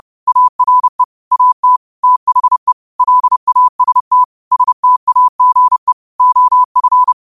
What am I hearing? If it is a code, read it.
Morse code, heard as a keyed tone, about 30 wpm, 1000 Hz. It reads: ARE AT THE LAST STAGE OF
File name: qotdfiles / 30wpm